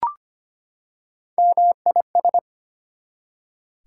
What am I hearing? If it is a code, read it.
Morse code, heard as a keyed tone, about 25 wpm, 700 Hz. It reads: MIS
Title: Prefixes and Suffixes 25wpm